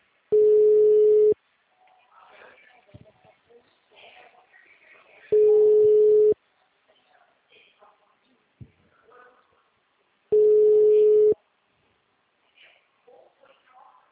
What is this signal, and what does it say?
Ringback tone (Europe, 425 Hz), with speech.
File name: out- -2006- - - .5373